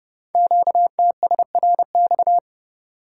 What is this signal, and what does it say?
Morse code, heard as a keyed tone, about 30 wpm, 700 Hz. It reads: QTSRX
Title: Code Groups - 5 Letters - 30wpm